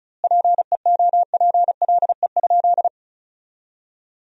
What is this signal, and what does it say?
Morse code, heard as a keyed tone, about 36 wpm, 700 Hz. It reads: PEOPLE?
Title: 2x Word-Spacing - Sentences from Top 200 Words - 35wpm